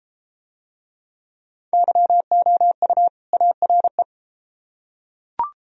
Morse code, heard as a keyed tone, about 30 wpm, 700 Hz. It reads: YOU ARE